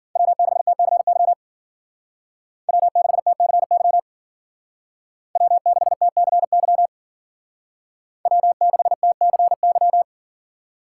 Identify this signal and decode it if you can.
Morse code, heard as a keyed tone, about 52 wpm, 700 Hz. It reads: W6TCY W6TCY W6TCY W6TCY